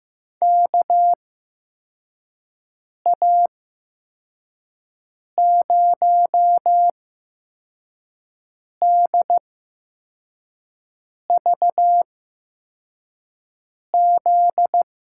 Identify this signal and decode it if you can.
Morse code, heard as a keyed tone, about 15 wpm, spaced as for 4 wpm, 700 Hz. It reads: KA0DVZ
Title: USA Call Signs - 6 Characters - 8x Character Spacing - 15wpm